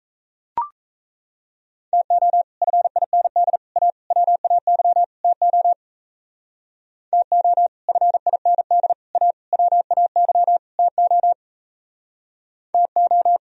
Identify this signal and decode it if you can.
Morse code, heard as a keyed tone, about 34 wpm, 700 Hz. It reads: TO FIND A WAY TO TO FIND A WAY TO TO
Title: Speed-Racing - Sets of 5 Words 28wpm